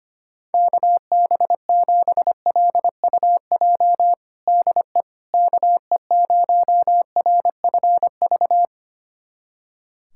Morse code, heard as a keyed tone, about 25 wpm, 700 Hz. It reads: KB7LUJ DE KE0RFV
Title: USA Callsign DE Callsign - 6 Characters - 25wpm